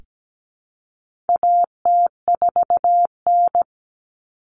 Morse code, heard as a keyed tone, about 17 wpm, 700 Hz. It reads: AT4N
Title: Common Cut Numbers - 4-Digit 17wpm